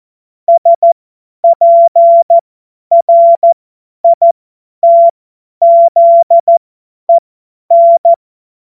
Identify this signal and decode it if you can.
Morse code, heard as a keyed tone, about 14 wpm, spaced as for 10 wpm, 680 Hz. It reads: SPRITZEN
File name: Morsekurs / Witze / Witz 27. April 2021, 50 F70